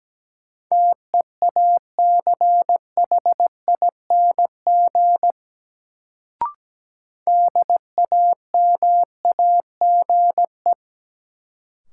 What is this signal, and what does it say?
Morse code, heard as a keyed tone, about 17 wpm, 700 Hz. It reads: TEACHING DAMAGE
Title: Sentences from Top 1100 Words - New Word Drill - 17wpm